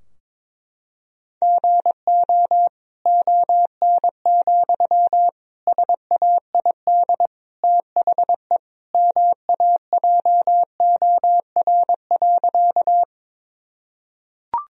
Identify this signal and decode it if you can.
Morse code, heard as a keyed tone, about 22 wpm, 700 Hz. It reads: GO ON, SAID THE MAJOR.
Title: Sentences from Top 1000 Words - Review Part 4 - 22wpm